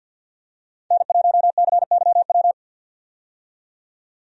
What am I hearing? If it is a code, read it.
Morse code, heard as a keyed tone, about 50 wpm, 700 Hz. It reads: N1CYW
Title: USA Call Signs - 5 Characters - 50wpm